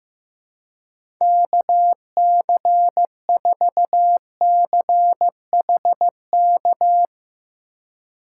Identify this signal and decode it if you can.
Morse code, heard as a keyed tone, about 16 wpm, 700 Hz. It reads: KC4CHK
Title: USA Call Signs - 8x Character Spacing - 15wpm